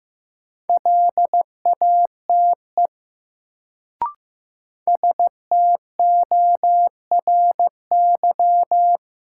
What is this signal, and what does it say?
Morse code, heard as a keyed tone, about 15 wpm, 700 Hz. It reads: LATE STORY
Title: Sets of 3 Words - New Word Drill - 15wpm